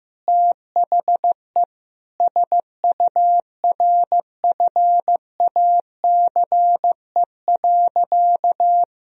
Morse code, heard as a keyed tone, about 15 wpm, 700 Hz. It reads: THE SURFACE.
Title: Sentences from Top 1000 Words - Review Part 3 - 15wpm